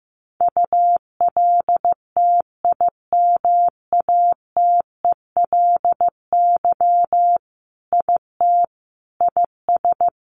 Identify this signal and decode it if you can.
Morse code, heard as a keyed tone, about 15 wpm, 700 Hz. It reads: ULTIMATELY IT IS
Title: Sentences from Top 1200 Words 15wpm